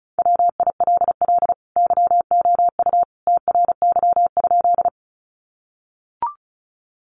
Morse code, heard as a keyed tone, about 34 wpm, 700 Hz. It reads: WILL YOU TRY?
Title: Sentences from Top 200 Words 35wpm